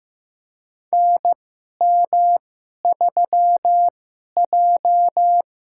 Morse code, heard as a keyed tone, about 15 wpm, spaced as for 11 wpm, 700 Hz. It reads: NM3J
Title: USA Call Signs - 2x Character Spacing - 15wpm